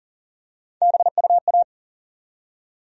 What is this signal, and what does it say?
Morse code, heard as a keyed tone, about 40 wpm, 700 Hz. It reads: BUA
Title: USA Call Sign Postfixes - 3 Letters 40wpm